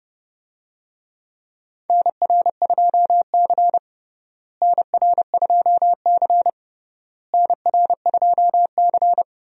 Morse code, heard as a keyed tone, about 32 wpm, 700 Hz. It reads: NR2C NR2C NR2C
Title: USA Call Signs - 4 Characters - Repeated 3x - 30wpm